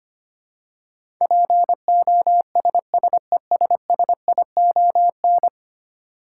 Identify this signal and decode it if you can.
Morse code, heard as a keyed tone, about 25 wpm, 700 Hz. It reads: POSSESSION
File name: 10 Letter Words - First 3 Letters Given - 25wpm